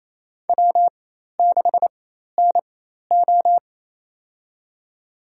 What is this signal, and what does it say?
Morse code, heard as a keyed tone, about 28 wpm, spaced as for 13 wpm, 700 Hz. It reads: W6NO
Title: USA Call Signs - 4 Characters - 4x Character Spacing - 28wpm